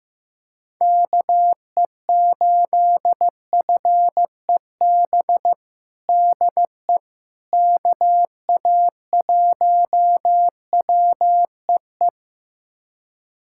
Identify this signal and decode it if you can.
Morse code, heard as a keyed tone, about 15 wpm, 700 Hz. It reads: KE8FEB DE KA1WEE
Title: USA Callsign DE Callsign - 6 Characters - 15wpm